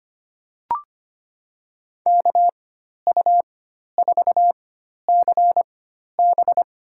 Morse code, heard as a keyed tone, about 25 wpm, spaced as for 12 wpm, 700 Hz. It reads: KU4CB
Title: USA Call Signs - 5 Characters - 4x Character Spacing - 25wpm